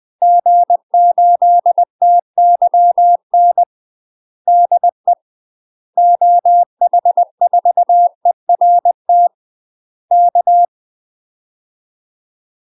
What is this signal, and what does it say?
Morse code, heard as a keyed tone, about 20 wpm, 700 Hz. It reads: G8TYN DE OH4ERT K
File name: UK MASTER QSOs TRAINING / 20WPM / UK MASTER QSOs 0012